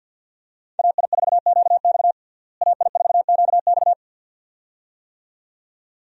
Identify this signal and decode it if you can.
Morse code, heard as a keyed tone, about 52 wpm, 700 Hz. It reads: AI4QX AI4QX
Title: USA Call Signs - 5 Characters - Repeated 2x - 50wpm